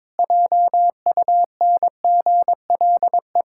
Morse code, heard as a keyed tone, about 23 wpm, 700 Hz. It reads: JUNGLE
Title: Pangram Challenge v2 - 22wpm